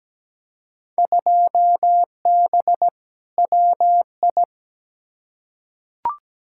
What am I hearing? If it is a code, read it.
Morse code, heard as a keyed tone, about 17 wpm, 700 Hz. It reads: 2B WI
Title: Field Day Exchanges - 17wpm